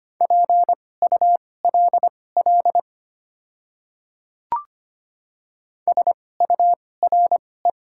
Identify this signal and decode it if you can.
Morse code, heard as a keyed tone, about 25 wpm, spaced as for 18 wpm, 700 Hz. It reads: PULL SURE
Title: 2x Character Spacing - Top 500 Words - 25wpm